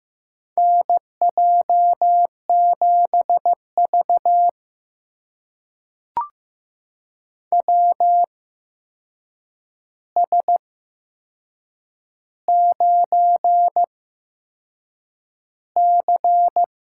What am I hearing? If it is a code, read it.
Morse code, heard as a keyed tone, about 15 wpm, spaced as for 6 wpm, 700 Hz. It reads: NJ7V WS9C
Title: USA Call Signs - 4 Characters - 8x Character Spacing - 15wpm